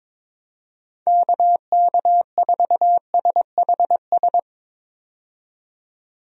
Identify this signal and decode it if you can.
Morse code, heard as a keyed tone, about 21 wpm, 700 Hz. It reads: KK4SHS
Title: USA Call Signs - 6 Characters - 22wpm